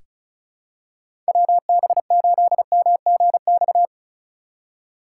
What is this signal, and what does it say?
Morse code, heard as a keyed tone, about 34 wpm, 700 Hz. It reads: WB8MGX
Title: USA Call Signs - 6 Characters - 3x Character Spacing - 35wpm